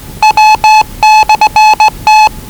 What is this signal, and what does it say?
Morse code, heard as a keyed tone, about 20 wpm, 840 Hz. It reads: W/T
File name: WellingtonRadioZlwCloseDownOnHfAndMf / 01-Zlw 4277khz 417.5 qrt until 0925z